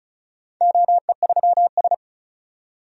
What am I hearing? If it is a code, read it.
Morse code, heard as a keyed tone, about 36 wpm, 700 Hz. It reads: OE3S
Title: DXCC-100 Call Signs - 4 Characters - 35wpm